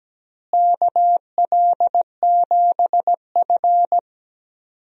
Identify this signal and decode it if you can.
Morse code, heard as a keyed tone, about 17 wpm, 700 Hz. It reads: KL7F
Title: DXCC-100 Call Signs - 4 Characters - Repeated 3x - 17wpm